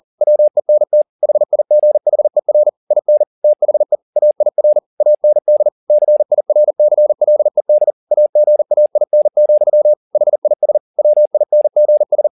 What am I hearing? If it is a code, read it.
Morse code, heard as a keyed tone, about 40 wpm, 600 Hz. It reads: WENT HIGHER IN THE AIR AND CIRCLED AGAIN, HIS WINGS